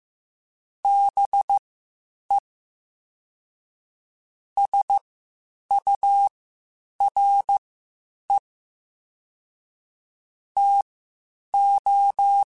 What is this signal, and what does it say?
Morse code, heard as a keyed tone, about 15 wpm, spaced as for 8 wpm, 800 Hz. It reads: BE SURE TO